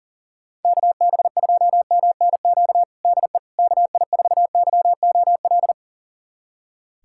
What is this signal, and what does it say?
Morse code, heard as a keyed tone, about 37 wpm, 690 Hz. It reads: KB2MNQ DE KI4YOL
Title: USA Callsign DE Callsign - 6 Characters - 40wpm